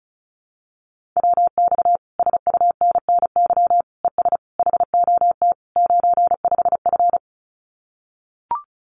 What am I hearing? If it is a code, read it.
Morse code, heard as a keyed tone, about 34 wpm, 700 Hz. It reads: WX SUNNY ES HOT 95F
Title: Common QSO Phrases Encore 35wpm